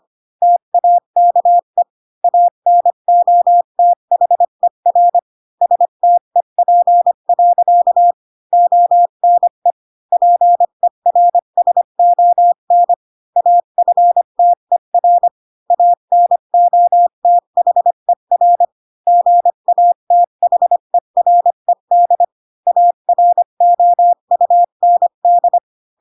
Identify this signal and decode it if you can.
Morse code, heard as a keyed tone, about 25 wpm, 700 Hz. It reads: TAKE ANOTHER STEP. ONE PERSON AFTER ANOTHER GATHERED AROUND
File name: Pinocchio / Pinocchio ch3 20 25 ik1hge